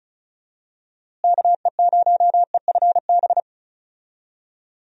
Morse code, heard as a keyed tone, about 36 wpm, 700 Hz. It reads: KE0EFB
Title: USA Call Signs - 6 Characters - Repeated 3x - 35wpm